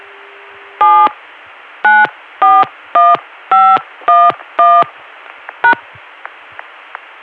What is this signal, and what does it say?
DTMF, heard as telephone touch tones, tones of about 240 ms. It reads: 7 941511 #